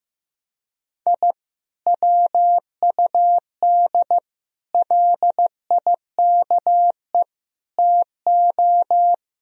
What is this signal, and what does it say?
Morse code, heard as a keyed tone, about 15 wpm, 700 Hz. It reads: I WUD LIKE TO
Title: Sentences from Top 200 Words and Abbreviations 15wpm